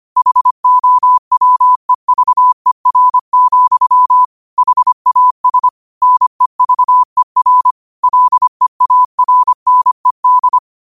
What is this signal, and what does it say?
Morse code, heard as a keyed tone, about 25 wpm, 1000 Hz. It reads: HOWEVER, HAS NEVER LEARNED
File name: 25wpm